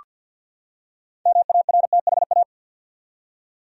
Morse code, heard as a keyed tone, about 47 wpm, 700 Hz. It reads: MARTHA
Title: USA Common Names - 50wpm